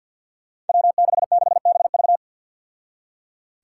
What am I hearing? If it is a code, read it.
Morse code, heard as a keyed tone, about 51 wpm, 700 Hz. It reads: W66BV
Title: Five Challenging Letter-Number Combinations 50wpm